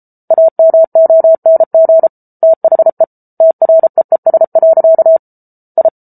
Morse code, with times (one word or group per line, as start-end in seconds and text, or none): 0.3-2.1 AMONG
2.4-3.0 THE
3.4-5.2 TREES.
5.8-5.9 I